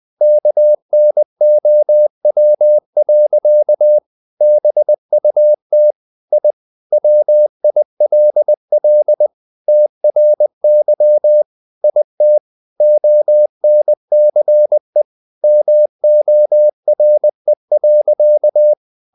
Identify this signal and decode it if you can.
Morse code, heard as a keyed tone, about 20 wpm, 600 Hz. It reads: KNOW. BUT I WILL TRY IT ONCE MORE.
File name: the old man and the sea part 2 20wpm